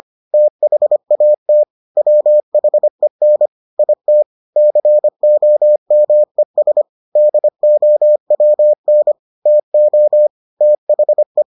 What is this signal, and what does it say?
Morse code, heard as a keyed tone, about 25 wpm, 600 Hz. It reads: THAT WHEN IT COMES DOWN TO THE